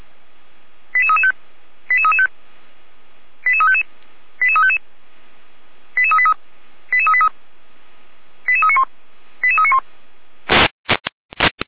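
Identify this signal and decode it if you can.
Five-tone selective call (ZVEI), heard as quick runs of five tones, tones of about 70 ms. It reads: 88335 88335 88355 88355 88383 88383 88391 88391
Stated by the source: ZVEI five-tone selective call 88365 received with an SDR.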